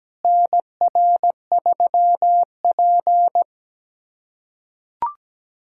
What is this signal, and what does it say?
Morse code, heard as a keyed tone, about 17 wpm, 700 Hz. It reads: NR3P
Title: USA Callsign DE Callsign - 4 Characters - 17wpm